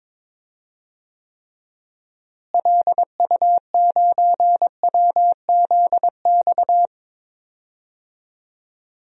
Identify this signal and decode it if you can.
Morse code, heard as a keyed tone, about 23 wpm, 700 Hz. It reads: LU9WZX
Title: DXCC-100 Call Signs - 6 Characters - Repeated 3x - 22wpm